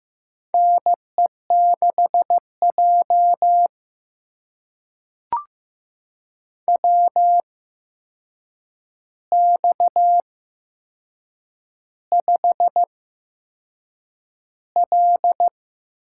Morse code, heard as a keyed tone, about 15 wpm, spaced as for 6 wpm, 700 Hz. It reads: NE6J WX5L
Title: USA Call Signs - 4 Characters - 8x Character Spacing - 15wpm